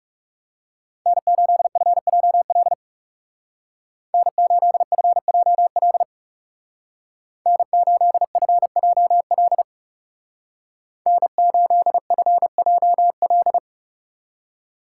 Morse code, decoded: N8FJL N8FJL N8FJL N8FJL

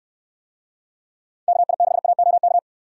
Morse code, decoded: KI6AOK

